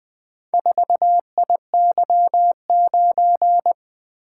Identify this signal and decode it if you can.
Morse code, heard as a keyed tone, about 24 wpm, 700 Hz. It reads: 4IY9
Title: Speed-Racing - Four Letter-Number Combinations - 20wpm